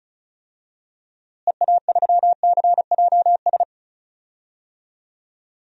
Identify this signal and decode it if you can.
Morse code, heard as a keyed tone, about 34 wpm, 700 Hz. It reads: EA3CJS
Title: DXCC-100 Call Signs - 6 Characters - Repeated 2x - 35wpm